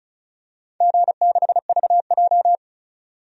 Morse code, heard as a keyed tone, about 34 wpm, 700 Hz. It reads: G6VJ
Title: DXCC-100 Call Signs - 4 Characters - 35wpm